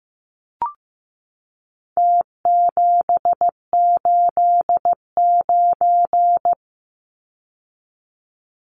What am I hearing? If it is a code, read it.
Morse code, heard as a keyed tone, about 15 wpm, 700 Hz. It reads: T789
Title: Numbers - Contest Style - 4-Digit 15wpm